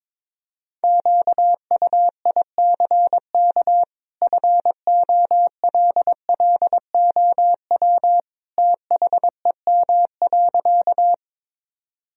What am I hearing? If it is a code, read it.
Morse code, heard as a keyed tone, about 22 wpm, 700 Hz. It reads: QUICK FOLLOW THEM.
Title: Sentences from Top 1000 Words - Review Part 1 - 22wpm